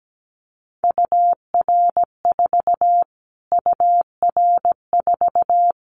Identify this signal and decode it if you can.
Morse code, heard as a keyed tone, about 17 wpm, 700 Hz. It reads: UR4 UR4
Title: Callsign Prefixes for Top 25 DXCC - Repeated 3x - 17wpm